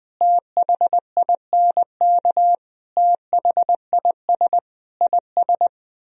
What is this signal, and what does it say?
Morse code, heard as a keyed tone, about 20 wpm, 700 Hz. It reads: THINK THIS IS
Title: Speed-Racing - Sets of 4 Words 17wpm